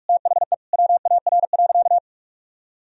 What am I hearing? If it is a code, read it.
Morse code, heard as a keyed tone, about 46 wpm, 700 Hz. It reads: THE WAR.
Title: Sentences from Top 400 Words and Abbreviations 45wpm